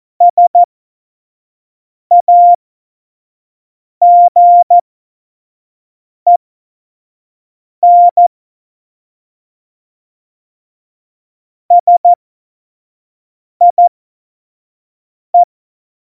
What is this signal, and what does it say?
Morse code, heard as a keyed tone, about 14 wpm, spaced as for 5 wpm, 700 Hz. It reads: SAGEN SIE
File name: Morsekurs / Witze / Witz 12 Mai 2020, 25 F70